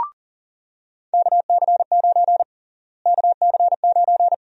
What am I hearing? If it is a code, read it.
Morse code, heard as a keyed tone, about 40 wpm, 700 Hz. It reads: KC9 KC9
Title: USA Call Sign Prefixes - Repeated 2x - 40wpm